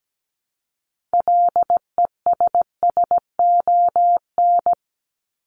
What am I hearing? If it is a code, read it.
Morse code, heard as a keyed tone, about 17 wpm, 700 Hz. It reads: LESSON
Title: Top 1001-1250 Words 17wpm